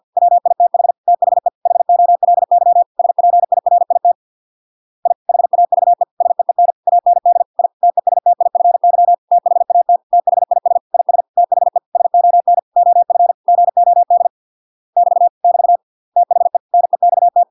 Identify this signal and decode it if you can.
Morse code, heard as a keyed tone, about 50 wpm, 700 Hz. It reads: WITH THE HOLY SPIRIT I HAVE SEEN AND I TESTIFY THAT THIS IS THE SON OF GOD <BT> <BT> THE NEXT